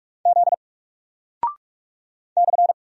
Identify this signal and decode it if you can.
Morse code, heard as a keyed tone, about 45 wpm, 700 Hz. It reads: Z /
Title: Single Character (Letter:Number:Punctuation:Prosign) Alt 45wpm